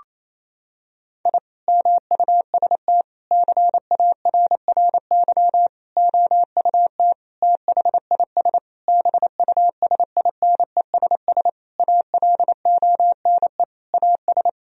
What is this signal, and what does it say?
Morse code, heard as a keyed tone, about 27 wpm, 700 Hz. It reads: I MUST CARRY OUT THIS BUSINESS ALONE AS
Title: Sentences from Top 1000 Words - Review Part 2 - 28wpm